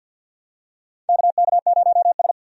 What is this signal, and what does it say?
Morse code, heard as a keyed tone, about 47 wpm, 700 Hz. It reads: KK0S
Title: USA Call Signs - 4 Characters - 50wpm